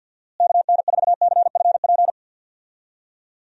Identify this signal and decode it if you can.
Morse code, heard as a keyed tone, about 46 wpm, 700 Hz. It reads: KN4CFP
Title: USA Call Signs - 6 Characters - Repeated 3x - 50wpm